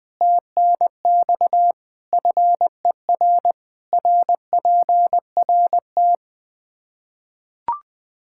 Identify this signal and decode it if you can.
Morse code, heard as a keyed tone, about 20 wpm, 700 Hz. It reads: TNX FER RPRT
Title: Practice QSO - Exchange 2 - Info - 20wpm